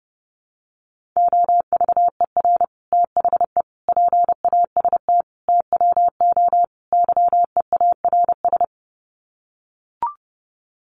Morse code, heard as a keyed tone, about 30 wpm, 700 Hz. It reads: OVER THE PAST TWO YEARS
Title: Sets of 5 Words Encore 30wpm